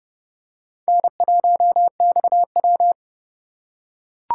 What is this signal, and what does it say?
Morse code, heard as a keyed tone, about 30 wpm, 700 Hz. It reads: N1XW
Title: USA Call Signs - 4 Characters - 4x Character Spacing - 30wpm